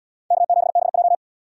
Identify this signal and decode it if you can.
Morse code, heard as a keyed tone, about 77 wpm, 700 Hz. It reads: K7BJ